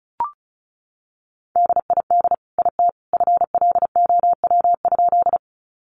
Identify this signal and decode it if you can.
Morse code, heard as a keyed tone, about 34 wpm, 700 Hz. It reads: DID IT FLOW?
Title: Sentences from Top 1200 Words 35wpm